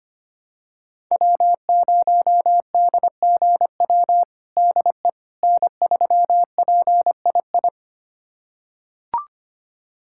Morse code, heard as a keyed tone, about 25 wpm, 700 Hz. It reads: W0DGW DE N3PII
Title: USA Callsign DE Callsign - 5 Characters - 25wpm